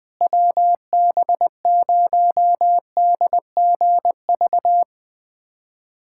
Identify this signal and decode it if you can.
Morse code, heard as a keyed tone, about 21 wpm, 700 Hz. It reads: WB0DGV
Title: USA Callsign DE Callsign 20wpm